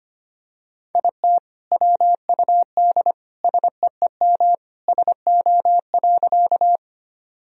Morse code, heard as a keyed tone, about 25 wpm, 700 Hz. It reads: IT WUD SEEM SO.
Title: Sentences from Top 300 Words and Abbreviations 25wpm